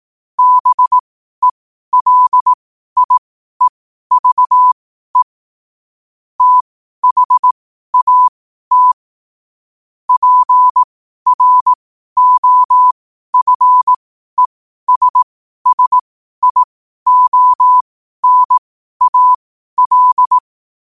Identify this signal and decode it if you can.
Morse code, heard as a keyed tone, about 18 wpm, spaced as for 12 wpm, 1000 Hz. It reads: BELIEVE THAT PROFESSIONAL
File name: qotdfiles / 12wpm